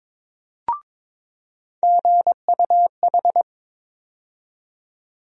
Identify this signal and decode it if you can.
Morse code, heard as a keyed tone, about 22 wpm, 700 Hz. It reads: GUH
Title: USA Call Sign Postfixes - 3 Letters 22wpm